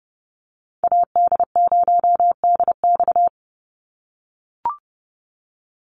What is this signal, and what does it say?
Morse code, heard as a keyed tone, about 30 wpm, 700 Hz. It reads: AD0DX
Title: Call Signs - 30wpm